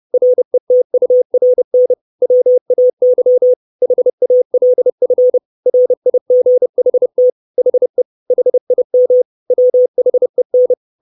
Morse code, heard as a keyed tone, about 30 wpm, 500 Hz. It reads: RETURN WAY HALF RIGHT HE HIM WHEN
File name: 632 Words 30WPM